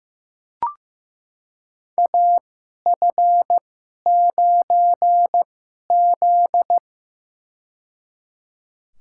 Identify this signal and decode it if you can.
Morse code, heard as a keyed tone, about 15 wpm, spaced as for 11 wpm, 700 Hz. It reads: AF9Z
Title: USA Call Signs - 4 Characters - 2x Character Spacing - 15wpm